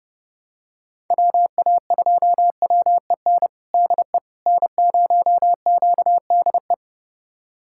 Morse code, heard as a keyed tone, about 32 wpm, 700 Hz. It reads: WA2WEN DE N0QDE
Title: USA Callsign DE Callsign 30wpm